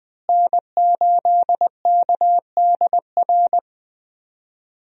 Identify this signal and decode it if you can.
Morse code, heard as a keyed tone, about 20 wpm, 700 Hz. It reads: N8KDR
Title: USA Call Signs - 5 Characters - Repeated 3x - 20wpm